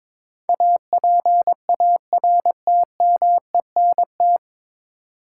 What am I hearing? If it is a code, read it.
Morse code, heard as a keyed tone, about 22 wpm, 700 Hz. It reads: APARTMENT
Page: Morse Code Ninja practice 22wpm